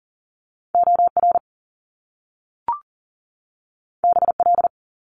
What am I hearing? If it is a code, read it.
Morse code, heard as a keyed tone, about 41 wpm, 700 Hz. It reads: OR BL
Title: Top Two Letter Combinations in Words 40wpm